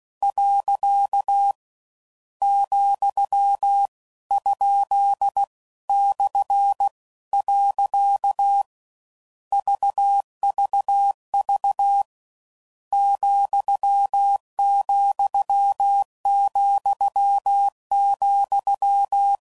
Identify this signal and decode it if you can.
Morse code, heard as a keyed tone, about 16 wpm, 800 Hz. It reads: . , ? /. VVV ,,,,